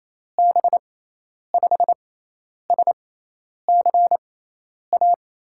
Morse code, heard as a keyed tone, about 28 wpm, spaced as for 10 wpm, 700 Hz. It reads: B5SCA
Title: USA Call Signs - 6 Characters - 6x Character Spacing - 28wpm